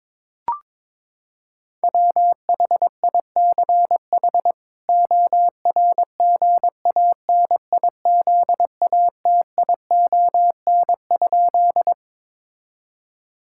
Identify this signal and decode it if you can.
Morse code, heard as a keyed tone, about 22 wpm, 700 Hz. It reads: WHICH ORGANIZATION?
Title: Sentences from Top 1000 Words - Review Part 5 - 22wpm